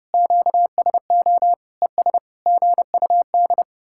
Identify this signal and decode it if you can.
Morse code, heard as a keyed tone, about 40 wpm, 700 Hz. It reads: QSO ES GUD